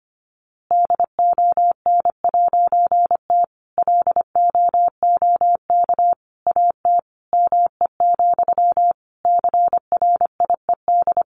Morse code, heard as a keyed tone, about 25 wpm, 700 Hz. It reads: DON'T LOOK AT ME, CRIED
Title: A Sound Of Thunder 25wpm